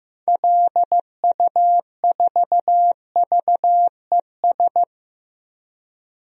Morse code, heard as a keyed tone, about 16 wpm, 700 Hz. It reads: LU4VES
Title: DXCC-100 Call Signs - 6 Characters - 15wpm